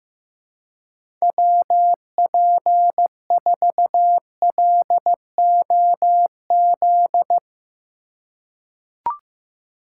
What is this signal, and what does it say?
Morse code, heard as a keyed tone, about 15 wpm, 700 Hz. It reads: WP4LOZ
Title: USA Call Signs - 6 Characters - 2x Character Spacing - 15wpm